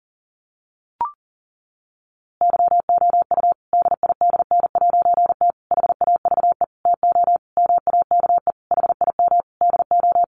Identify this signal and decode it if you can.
Morse code, heard as a keyed tone, about 40 wpm, 700 Hz. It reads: YOU DIDN'T HAVE TO MAKE HIM DO
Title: A Sound Of Thunder 40wpm